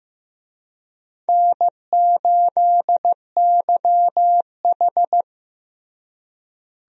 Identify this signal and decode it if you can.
Morse code, heard as a keyed tone, about 15 wpm, 700 Hz. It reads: N8YH
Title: USA Call Signs - 4 Characters - Repeated 2x - 15wpm